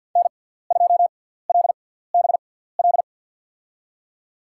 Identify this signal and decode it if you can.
Morse code, heard as a keyed tone, about 53 wpm, spaced as for 17 wpm, 700 Hz. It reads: N2LBL
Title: USA Call Signs - 6x Character Spacing - 50wpm